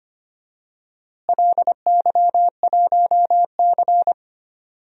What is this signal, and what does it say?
Morse code, heard as a keyed tone, about 25 wpm, 700 Hz. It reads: LY1C
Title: DXCC-100 Call Signs - 4 Characters - Repeated 3x - 25wpm